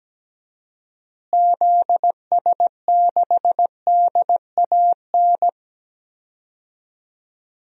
Morse code, heard as a keyed tone, about 17 wpm, 700 Hz. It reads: ZS6DAN
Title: DXCC-100 Call Signs - 6 Characters - Repeated 3x - 17wpm